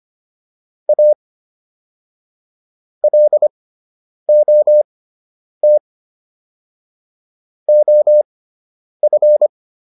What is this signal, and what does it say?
Morse code, heard as a keyed tone, about 25 wpm, spaced as for 9 wpm, 600 Hz. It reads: A LOT OF